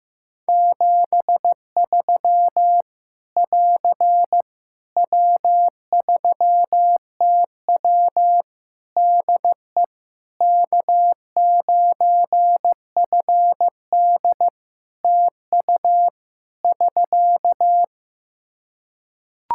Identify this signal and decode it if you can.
Morse code, heard as a keyed tone, about 15 wpm, 700 Hz. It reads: 73 <AR> W3TW DE K9FD TU <SK>
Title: Practice QSO - Exchange 4 - Ending - 15wpm